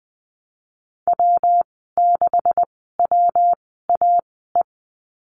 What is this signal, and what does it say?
Morse code, heard as a keyed tone, about 19 wpm, spaced as for 14 wpm, 700 Hz. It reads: W6WAE
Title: 173 Lesson 28g - Number 9 - Call Signs 20wpm